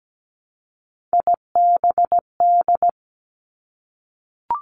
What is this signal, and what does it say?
Morse code, heard as a keyed tone, about 17 wpm, 700 Hz. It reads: IBD
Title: Three Challenging Letter-Number Combinations 17wpm